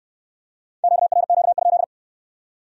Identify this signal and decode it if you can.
Morse code, heard as a keyed tone, about 68 wpm, 700 Hz. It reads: QRQ?